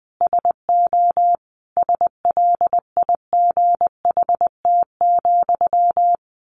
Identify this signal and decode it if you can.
Morse code, heard as a keyed tone, about 20 wpm, 700 Hz. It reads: SO SLIGHT,
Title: A Sound Of Thunder 20wpm